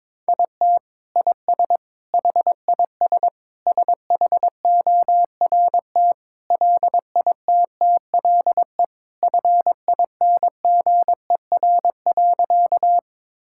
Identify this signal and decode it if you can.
Morse code, heard as a keyed tone, about 22 wpm, 700 Hz. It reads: IT IS HIS SHORT LITTLE FINGER.
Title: Sentences from Top 1600 Words 22wpm